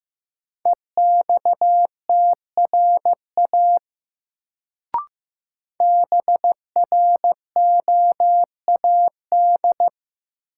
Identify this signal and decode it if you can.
Morse code, heard as a keyed tone, about 15 wpm, 700 Hz. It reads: EXTRA BROAD
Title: Sentences from Top 1000 Words - New Word Drill - 15wpm